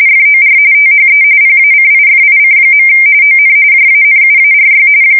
RTTY, about 45.45 baud. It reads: FOR THE PACIFIC. ON JANUARY
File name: fsk170 / battleshipnj